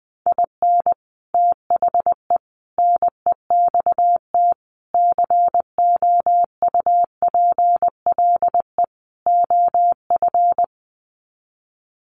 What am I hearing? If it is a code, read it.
Morse code, heard as a keyed tone, about 20 wpm, 700 Hz. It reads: IN THE NEXT COUPLE OF
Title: Sets of 5 Words Encore2 20wpm